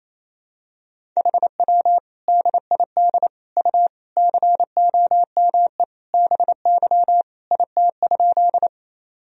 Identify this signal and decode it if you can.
Morse code, heard as a keyed tone, about 28 wpm, 700 Hz. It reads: HW DID U COME BY IT?
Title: Sentences from Top 100 Words and Abbreviations 28wpm